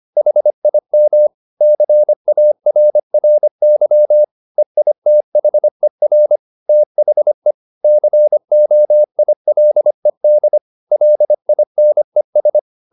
Morse code, heard as a keyed tone, about 25 wpm, 600 Hz. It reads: HIM CARRY EITHER THE COILED LINES
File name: the old man and the sea part 1 25wpm